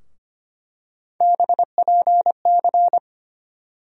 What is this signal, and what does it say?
Morse code, heard as a keyed tone, about 25 wpm, 700 Hz. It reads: BPC
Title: USA Call Sign Postfixes - 3 Letters 25wpm